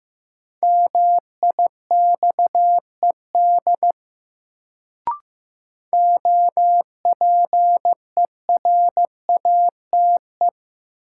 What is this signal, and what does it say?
Morse code, heard as a keyed tone, about 16 wpm, 700 Hz. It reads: MIXED OPERATE
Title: Sentences from Top 1500 Words - New Word Drill - 15wpm